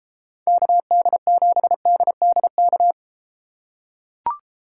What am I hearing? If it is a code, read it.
Morse code, heard as a keyed tone, about 31 wpm, 700 Hz. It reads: KD7DDK